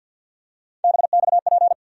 Morse code, heard as a keyed tone, about 46 wpm, 690 Hz. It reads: BXP
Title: USA Call Sign Postfixes - 3 Letters 50wpm